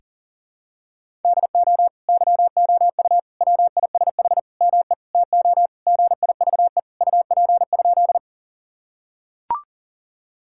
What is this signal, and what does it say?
Morse code, heard as a keyed tone, about 40 wpm, 700 Hz. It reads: DO YOU WISH ME TO GIVE UP?
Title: Sentences from Top 1600 Words 40wpm